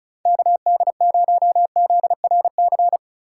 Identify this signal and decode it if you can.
Morse code, heard as a keyed tone, about 34 wpm, 700 Hz. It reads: KD0ZRC
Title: USA Call Signs - 6 Characters - 4x Character Spacing - 35wpm